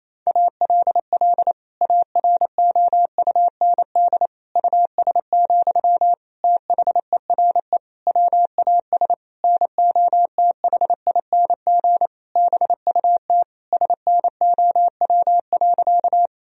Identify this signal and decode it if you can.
Morse code, heard as a keyed tone, about 28 wpm, 700 Hz. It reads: ALL AROUND US, THERE WAS NOTHING BUT SNOW.
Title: Sentences from Top 1000 Words - Review Part 3 - 28wpm